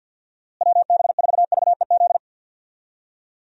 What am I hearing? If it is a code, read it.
Morse code, heard as a keyed tone, about 47 wpm, 700 Hz. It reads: WB4VEZ
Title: USA Call Signs - 6 Characters - Repeated 2x - 50wpm